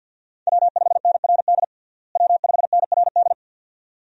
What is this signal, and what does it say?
Morse code, heard as a keyed tone, about 50 wpm, 700 Hz. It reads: W5NRD W5NRD
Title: USA Call Signs - 5 Characters - Repeated 2x - 50wpm